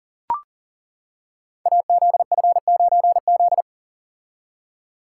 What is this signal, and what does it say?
Morse code, heard as a keyed tone, about 38 wpm, 690 Hz. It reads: AZF9Z
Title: Code Groups - 5 Letter:Numbers - 40wpm